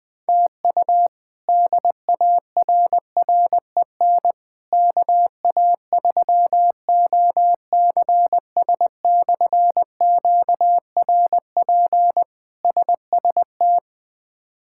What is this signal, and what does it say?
Morse code, heard as a keyed tone, about 20 wpm, 700 Hz. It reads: TU DARREN KA3OCS/QRP SST
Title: K1USN SST QSO Practice - 20wpm